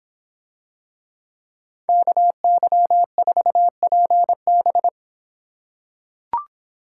Morse code, heard as a keyed tone, about 26 wpm, 700 Hz. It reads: KY4PB